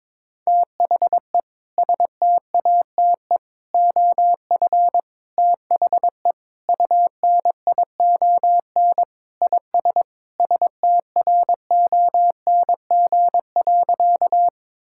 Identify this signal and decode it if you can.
Morse code, heard as a keyed tone, about 22 wpm, 700 Hz. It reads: THE STATE OF THE UNION IS STRONG.
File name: practice / 22wpm / Sentences with 3-Word Phrases - 22wpm